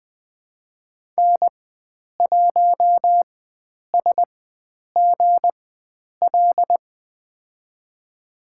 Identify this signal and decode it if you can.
Morse code, heard as a keyed tone, about 20 wpm, spaced as for 9 wpm, 700 Hz. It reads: N1SGL
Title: USA Call Signs - 5 Characters - 4x Character Spacing - 20wpm